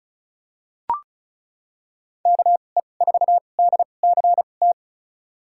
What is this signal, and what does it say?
Morse code, heard as a keyed tone, about 34 wpm, spaced as for 25 wpm, 700 Hz. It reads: KE4DCT
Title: USA Call Signs - 6 Characters - 2x Character Spacing - 35wpm